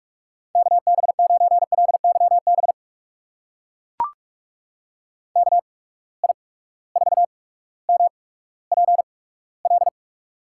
Morse code, heard as a keyed tone, about 44 wpm, spaced as for 19 wpm, 700 Hz. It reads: KB9LYB KI4MPL
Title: USA Call Signs - 6 Characters - 8x Character Spacing - 45wpm